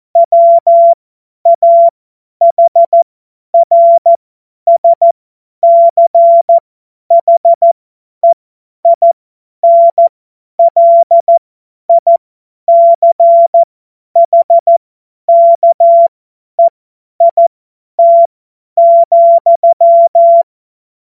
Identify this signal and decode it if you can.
Morse code, heard as a keyed tone, about 14 wpm, spaced as for 10 wpm, 680 Hz. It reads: WAHRSCHEINLICHKEIT,
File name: Morsekurs / Witze / Witz 05. April 2022 50 F70